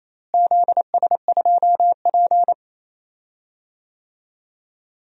Morse code, heard as a keyed tone, about 28 wpm, 700 Hz. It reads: ZS2P
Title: DXCC-100 Call Signs - 4 Characters - Repeated 2x - 28wpm